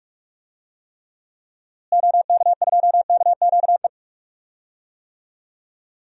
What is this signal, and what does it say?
Morse code, heard as a keyed tone, about 43 wpm, 700 Hz. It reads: OK2KQE
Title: DXCC-100 Call Signs - 6 Characters - Repeated 2x - 45wpm